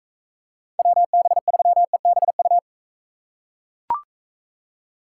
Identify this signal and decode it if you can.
Morse code, heard as a keyed tone, about 44 wpm, 700 Hz. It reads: WB3EBU